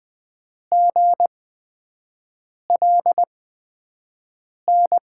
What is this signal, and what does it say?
Morse code, heard as a keyed tone, about 20 wpm, spaced as for 5 wpm, 700 Hz. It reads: GLN
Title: USA Call Signs - 8x Character Spacing - 20wpm